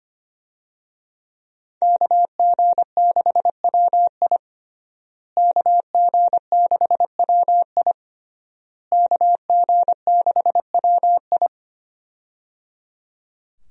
Morse code, heard as a keyed tone, about 25 wpm, 700 Hz. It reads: KG6WI KG6WI KG6WI
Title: USA Call Signs - 5 Characters - Repeated 3x - 25wpm